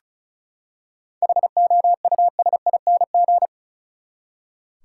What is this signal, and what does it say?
Morse code, heard as a keyed tone, about 36 wpm, 700 Hz. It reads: HOUSING
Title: Seven Letter Words - 35wpm